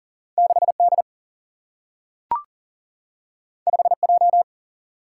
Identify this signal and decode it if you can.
Morse code, heard as a keyed tone, about 39 wpm, 700 Hz. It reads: 6D 5J
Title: Single-Letter Single-Number 40wpm